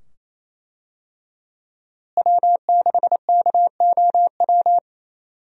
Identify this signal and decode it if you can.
Morse code, heard as a keyed tone, about 28 wpm, 700 Hz. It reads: W6KOW
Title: USA Call Signs - 5 Characters - Repeated 2x - 28wpm